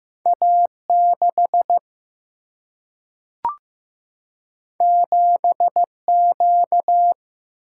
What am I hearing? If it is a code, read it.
Morse code, heard as a keyed tone, about 15 wpm, 700 Hz. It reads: A6 7Q
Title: Single-Letter Single-Number 15wpm